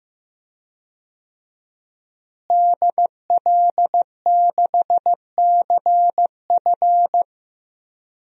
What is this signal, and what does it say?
Morse code, heard as a keyed tone, about 15 wpm, 700 Hz. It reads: DL6CF
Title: DXCC-100 Call Signs - 5 Characters - Repeated 2x - 15wpm